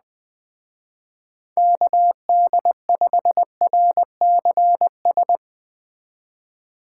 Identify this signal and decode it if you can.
Morse code, heard as a keyed tone, about 21 wpm, 700 Hz. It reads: KD5RCS